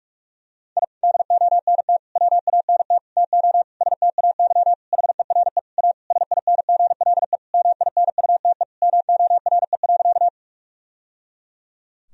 Morse code, heard as a keyed tone, about 46 wpm, 700 Hz. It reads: I DONT WANT TO STAY HERE A SINGLE MINUTE MORE.
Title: Sentences from Top 1000 Words - Review Part 4 - 45wpm